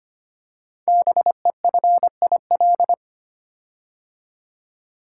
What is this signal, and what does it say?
Morse code, heard as a keyed tone, about 25 wpm, 700 Hz. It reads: BEFIL
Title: Code Groups - 5 Letters - 25wpm